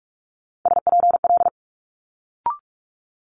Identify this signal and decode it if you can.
Morse code, heard as a keyed tone, about 46 wpm, 700 Hz. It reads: SPL
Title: Multi-letter Phonemes Quick Fire 45wpm